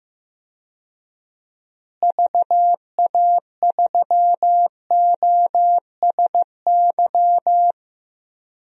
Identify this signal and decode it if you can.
Morse code, heard as a keyed tone, about 15 wpm, 700 Hz. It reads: VA3OSY
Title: DXCC-100 Call Signs - 6 Characters - Repeated 2x - 15wpm